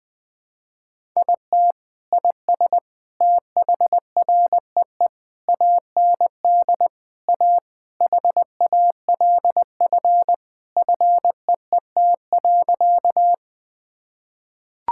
Morse code, heard as a keyed tone, about 20 wpm, 700 Hz. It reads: IT IS THREE AND A HALF FEET.
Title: Sentences with 3-Word Phrases - 20wpm